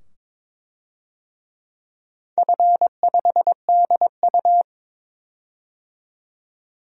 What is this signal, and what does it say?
Morse code, heard as a keyed tone, about 23 wpm, 700 Hz. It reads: F5DU
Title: DXCC-100 Call Signs - 4 Characters - 22wpm